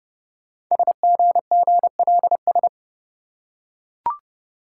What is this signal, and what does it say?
Morse code, heard as a keyed tone, about 30 wpm, 700 Hz. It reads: SGGLS
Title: Code Groups - 5 Letters - 30wpm